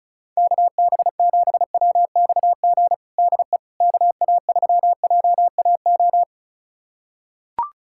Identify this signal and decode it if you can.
Morse code, heard as a keyed tone, about 36 wpm, 700 Hz. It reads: KB7WXG DE KA3JAO
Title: USA Callsign DE Callsign 35wpm